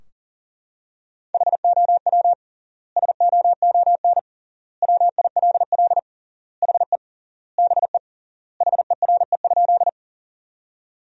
Morse code, heard as a keyed tone, about 37 wpm, 700 Hz. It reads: HOW SOON WILL HE BE HERE?
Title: 3x Word-Spacing - Sentences from Top 400 Words - 40wpm